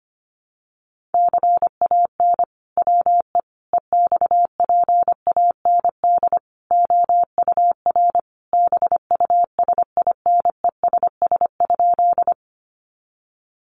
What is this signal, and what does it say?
Morse code, heard as a keyed tone, about 25 wpm, 700 Hz. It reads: CAN WE EXPAND OUR BUSINESS?
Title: Sentences from Top 1400 Words 25wpm